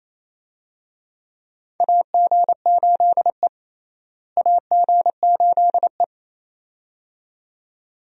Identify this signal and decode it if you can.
Morse code, heard as a keyed tone, about 28 wpm, 700 Hz. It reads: AG8E AG8E
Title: USA Call Signs - 4 Characters - Repeated 2x - 28wpm